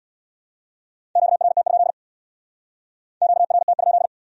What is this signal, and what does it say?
Morse code, heard as a keyed tone, about 60 wpm, 700 Hz. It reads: QRT? QRT?
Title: Speed-Racing Q-Codes - 50wpm